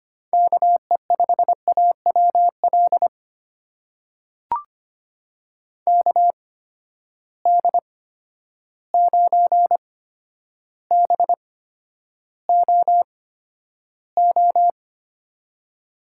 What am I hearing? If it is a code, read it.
Morse code, heard as a keyed tone, about 25 wpm, spaced as for 11 wpm, 700 Hz. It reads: KE5AWL KD9BOO
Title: USA Call Signs - 6 Characters - 8x Character Spacing - 25wpm